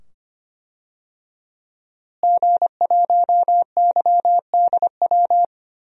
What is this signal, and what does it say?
Morse code, heard as a keyed tone, about 25 wpm, 700 Hz. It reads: G1YDW
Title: DXCC-100 Call Signs - 5 Characters - Repeated 2x - 25wpm